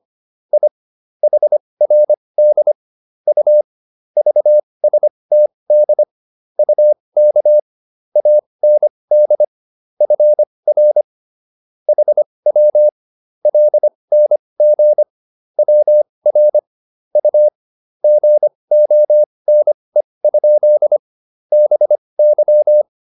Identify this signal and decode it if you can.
Morse code, heard as a keyed tone, about 25 wpm, spaced as for 20 wpm, 600 Hz. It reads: I HRD U VSTD UK AND FR HW LNG WR U GONE? BY